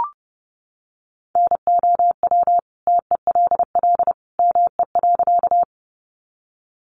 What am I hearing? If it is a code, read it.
Morse code, heard as a keyed tone, about 30 wpm, 700 Hz. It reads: NOW TELL ME.
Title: Sentences from Top 200 Words 30wpm